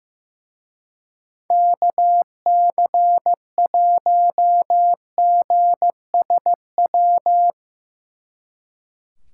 Morse code, heard as a keyed tone, about 16 wpm, 700 Hz. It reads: KC1GSW
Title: Speed Racing - USA Call Signs 15wpm